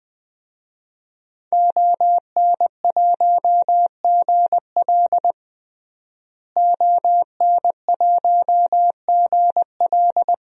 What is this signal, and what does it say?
Morse code, heard as a keyed tone, about 20 wpm, 700 Hz. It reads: ON1GL ON1GL
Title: DXCC-100 Call Signs - 5 Characters - Repeated 2x - 20wpm